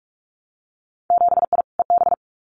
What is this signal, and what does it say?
Morse code, heard as a keyed tone, about 46 wpm, 700 Hz. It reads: 7I EB
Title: Winter Field Day Exchanges Encore 45wpm